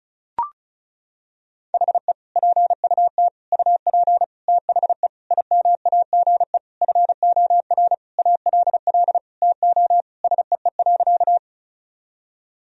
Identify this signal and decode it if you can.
Morse code, heard as a keyed tone, about 36 wpm, 700 Hz. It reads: HE PUT UP THE IMAGE FOR ALL TO SEE.
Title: Sentences from Top 1000 Words - Review Part 4 - 35wpm